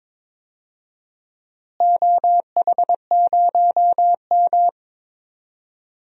Morse code, heard as a keyed tone, about 22 wpm, 700 Hz. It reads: OH0M
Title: DXCC-100 Call Signs - 4 Characters - Repeated 3x - 22wpm